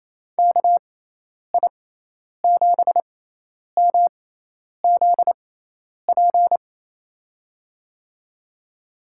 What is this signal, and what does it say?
Morse code, heard as a keyed tone, about 28 wpm, spaced as for 10 wpm, 700 Hz. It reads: KI7MZP
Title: USA Call Signs - 6 Characters - 6x Character Spacing - 28wpm